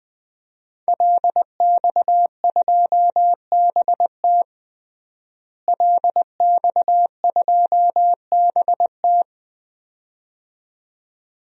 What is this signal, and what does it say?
Morse code, heard as a keyed tone, about 20 wpm, 700 Hz. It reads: LX2BT LX2BT
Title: DXCC-100 Call Signs - 5 Characters - Repeated 2x - 20wpm